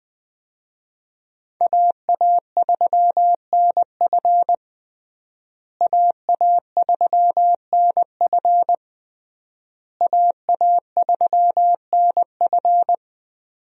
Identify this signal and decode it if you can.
Morse code, heard as a keyed tone, about 20 wpm, 700 Hz. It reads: AA3NF AA3NF AA3NF
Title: USA Call Signs - 5 Characters - Repeated 3x - 20wpm